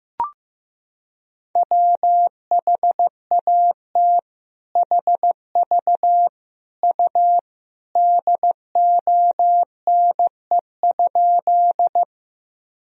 Morse code, heard as a keyed tone, about 15 wpm, 700 Hz. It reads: WHAT HV U DONE?
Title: Sentences from Top 500 Words and Abbreviations 15wpm